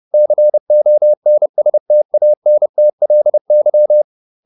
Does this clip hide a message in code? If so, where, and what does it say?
Yes: Morse code from 0.1 to 4.0 s, heard as a keyed tone, about 30 wpm, 600 Hz. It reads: CONSTANTLY